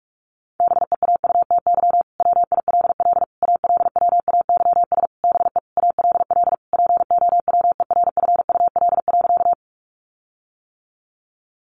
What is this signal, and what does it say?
Morse code, heard as a keyed tone, about 44 wpm, 700 Hz. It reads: BEAUTY WILL ALWAYS BE ALL POWERFUL.
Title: Sentences from Top 1300 Words 45wpm